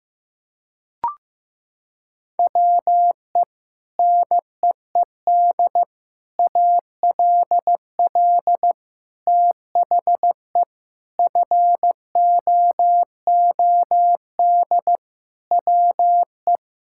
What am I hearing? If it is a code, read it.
Morse code, heard as a keyed tone, about 15 wpm, 700 Hz. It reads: WE NEED ALL THE FOOD WE
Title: Sentences from Top 1000 Words - Review Part 5 - 15wpm